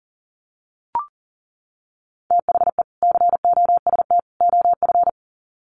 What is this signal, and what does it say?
Morse code, heard as a keyed tone, about 40 wpm, 700 Hz. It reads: THE COST OF
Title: Sets of 3 Words 40wpm